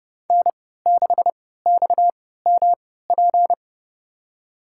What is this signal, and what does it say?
Morse code, heard as a keyed tone, about 30 wpm, spaced as for 17 wpm, 700 Hz. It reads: N6XMP
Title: USA Call Signs - 3x Character Spacing - 30wpm